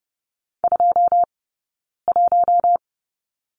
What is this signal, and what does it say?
Morse code, heard as a keyed tone, about 30 wpm, spaced as for 9 wpm, 690 Hz. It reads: 21
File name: Lesson 18d - Letter m - Two Characters 30wpm